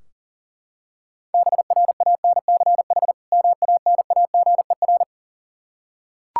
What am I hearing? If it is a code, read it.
Morse code, heard as a keyed tone, about 40 wpm, 700 Hz. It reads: BRANCH MANAGER
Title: Practice QSO - Exchange 3 - More Info - 40wpm